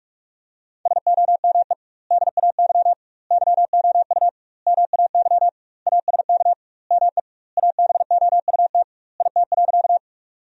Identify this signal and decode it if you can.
Morse code, heard as a keyed tone, about 45 wpm, 700 Hz. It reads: SOME DAY YOU MAY ASK ME ABOUT IT.
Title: 2x Word-Spacing - Sentences from Top 200 Words - 45wpm